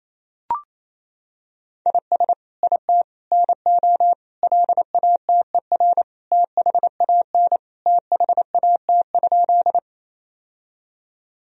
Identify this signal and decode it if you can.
Morse code, heard as a keyed tone, about 27 wpm, 700 Hz. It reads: IS IT NO LATER THAN THAT?
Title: Sentences from Top 1000 Words - Review Part 4 - 28wpm